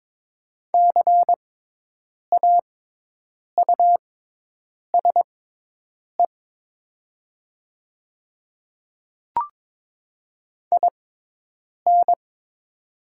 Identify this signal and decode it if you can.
Morse code, heard as a keyed tone, about 22 wpm, spaced as for 8 wpm, 700 Hz. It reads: CAUSE IN
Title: 6x Character Spacing - Top 300 Words - 22wpm